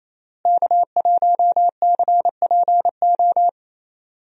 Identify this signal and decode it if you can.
Morse code, heard as a keyed tone, about 28 wpm, 700 Hz. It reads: K1CPO
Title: USA Call Signs - 5 Characters - Repeated 2x - 28wpm